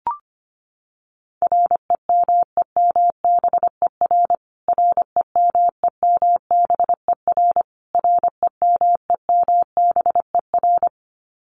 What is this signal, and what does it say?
Morse code, heard as a keyed tone, about 25 wpm, 700 Hz. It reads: REMEMBER REMEMBER REMEMBER
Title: Top 500 Words Repeated 25wpm